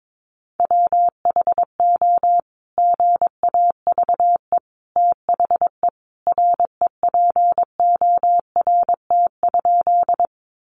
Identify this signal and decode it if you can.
Morse code, heard as a keyed tone, about 22 wpm, 700 Hz. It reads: WHO GAVE THE REPORT?
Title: Sentences from Top 900 Words 22wpm